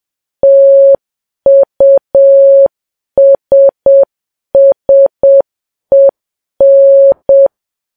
Morse code, heard as a keyed tone, about 7 wpm, 550 Hz. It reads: TUSSEN
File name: Files Web tls / Morse on9fv / morse7kps